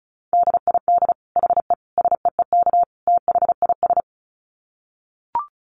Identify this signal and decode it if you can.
Morse code, heard as a keyed tone, about 34 wpm, 700 Hz. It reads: DID HE SEEK THIS
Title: Dit Challenge 35wpm